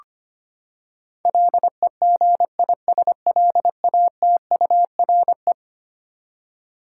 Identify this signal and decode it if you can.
Morse code, heard as a keyed tone, about 25 wpm, 700 Hz. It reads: LEGISLATURE
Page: morse-code-ninja-practice-25wpm